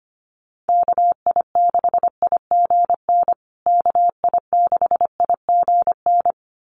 Morse code, heard as a keyed tone, about 25 wpm, 700 Hz. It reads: KI6IGN KI6IGN
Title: USA Call Signs Repeated 2x 25wpm